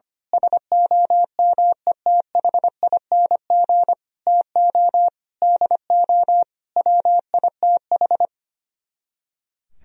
Morse code, heard as a keyed tone, about 24 wpm, 700 Hz. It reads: SOMETHING TO DO WITH
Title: Speed-Racing - Sets of 5 Words 25wpm